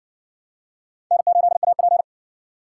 Morse code, heard as a keyed tone, about 59 wpm, 700 Hz. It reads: N8AP